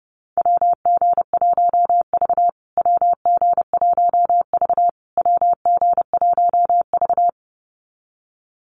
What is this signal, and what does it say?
Morse code, heard as a keyed tone, about 28 wpm, 700 Hz. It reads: WG1V WG1V WG1V
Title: USA Call Signs Repeated 3x 30wpm